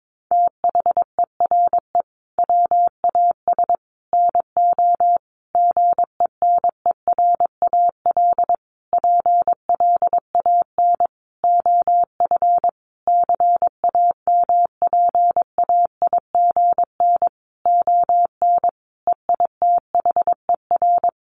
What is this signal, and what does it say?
Morse code, heard as a keyed tone, about 22 wpm, 700 Hz. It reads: THERE WAS NO GENERAL PLAN OF CAMPAIGN ON EITHER
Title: Sentences from Top 1200 Words 22wpm